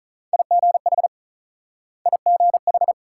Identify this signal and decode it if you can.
Morse code, heard as a keyed tone, about 48 wpm, 700 Hz. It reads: IGH IGH